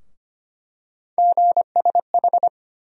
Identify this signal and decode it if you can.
Morse code, heard as a keyed tone, about 25 wpm, 700 Hz. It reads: GSH
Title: USA Call Sign Postfixes - 3 Letters 25wpm